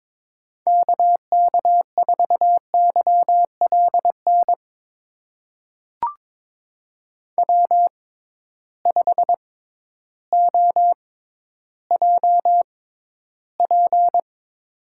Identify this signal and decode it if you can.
Morse code, heard as a keyed tone, about 22 wpm, spaced as for 12 wpm, 700 Hz. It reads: KK4YLN W5OJP
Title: USA Call Signs - 6x Character Spacing - 22wpm